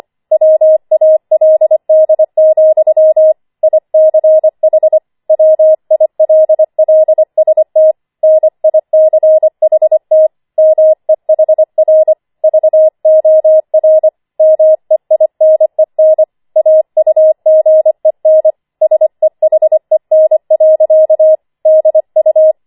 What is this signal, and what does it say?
Morse code, heard as a keyed tone, about 25 wpm, 630 Hz. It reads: WALD, ICH WILLST NICHT MEHR VOR MEINEN AUGEN SEHEN. DU
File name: 25wpm